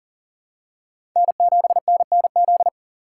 Morse code, heard as a keyed tone, about 38 wpm, 700 Hz. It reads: N7NNZ
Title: USA Call Signs - 5 Characters - 3x Character Spacing - 40wpm